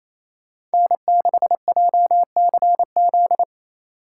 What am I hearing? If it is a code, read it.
Morse code, heard as a keyed tone, about 33 wpm, 700 Hz. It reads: N6JCZ
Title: Speed Racing - USA 1x3 Call Signs 28wpm